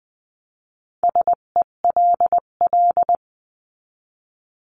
Morse code, heard as a keyed tone, about 20 wpm, 700 Hz. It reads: SELL
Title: 258 Lesson 42c - Words 16-20wpm